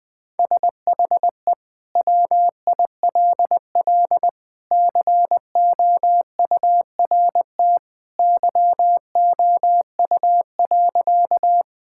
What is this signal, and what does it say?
Morse code, heard as a keyed tone, about 20 wpm, 700 Hz. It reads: SHE WILL COURT YOU.
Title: Sentences from Top 1600 Words 20wpm